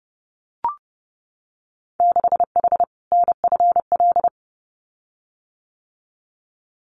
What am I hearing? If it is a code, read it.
Morse code, heard as a keyed tone, about 30 wpm, 700 Hz. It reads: 6H NFL
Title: Winter Field Day Exchanges Encore 30wpm